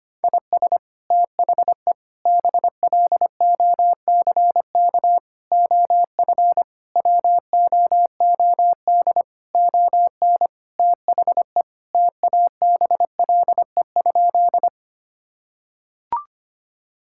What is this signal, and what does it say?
Morse code, heard as a keyed tone, about 25 wpm, 700 Hz. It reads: IS THE BLOCK OF WOOD ON THE TABLE?
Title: Sentences from Top 1600 Words 25wpm